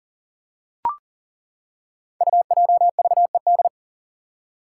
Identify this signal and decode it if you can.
Morse code, heard as a keyed tone, about 40 wpm, 700 Hz. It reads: UJVED
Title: Five Challenging Letter-Number Combinations 40wpm